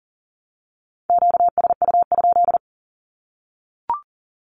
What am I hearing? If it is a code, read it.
Morse code, heard as a keyed tone, about 40 wpm, 700 Hz. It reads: QSU?
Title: Q-Codes Repeated 3x 40wpm